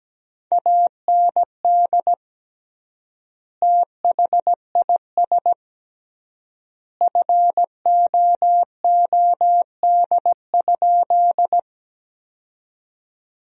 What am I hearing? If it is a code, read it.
Morse code, heard as a keyed tone, about 17 wpm, 700 Hz. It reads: AND THIS FOOD?
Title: 3x Word-Spacing - Sentences from Top 300 Words - 17wpm